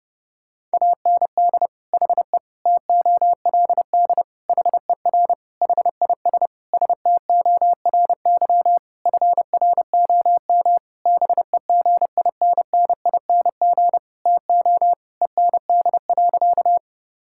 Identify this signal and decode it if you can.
Morse code, heard as a keyed tone, about 30 wpm, 700 Hz. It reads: AND HE TOLD HER HIS STORY FROM BEGINNING TO END.
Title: Sentences from Top 1000 Words - Review Part 5 - 30wpm